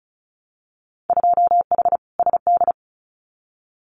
Morse code, heard as a keyed tone, about 34 wpm, 700 Hz. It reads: 2H SD
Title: Winter Field Day Exchanges 35wpm